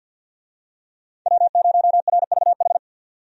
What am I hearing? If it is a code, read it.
Morse code, heard as a keyed tone, about 52 wpm, 700 Hz. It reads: W0RVH
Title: USA Call Signs - 5 Characters - 50wpm